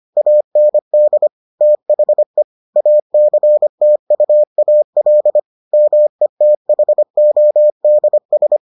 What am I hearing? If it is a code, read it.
Morse code, heard as a keyed tone, about 25 wpm, 600 Hz. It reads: AND THE ACTUAL METHODS